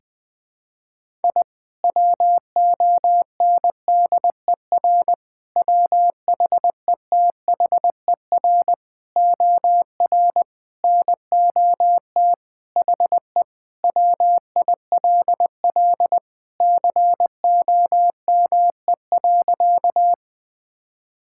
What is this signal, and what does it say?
Morse code, heard as a keyed tone, about 20 wpm, 700 Hz. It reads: I WONDER WHETHER OR NOT HE WILL COME.
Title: Sentences with 3-Word Phrases - Spoken Beginning - 20wpm